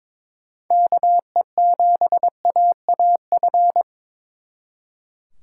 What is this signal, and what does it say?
Morse code, heard as a keyed tone, about 25 wpm, 700 Hz. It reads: KE7AAF
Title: Speed Racing - USA 2x3 Call Signs 22wpm